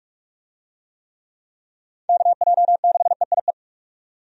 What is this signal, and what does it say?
Morse code, heard as a keyed tone, about 46 wpm, 700 Hz. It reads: KJ6EIE
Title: USA Call Signs - 6 Characters - 45wpm